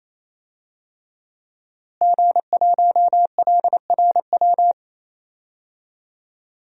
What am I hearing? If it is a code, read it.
Morse code, heard as a keyed tone, about 28 wpm, 700 Hz. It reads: G1LRW
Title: DXCC-100 Call Signs - 5 Characters - Repeated 3x - 28wpm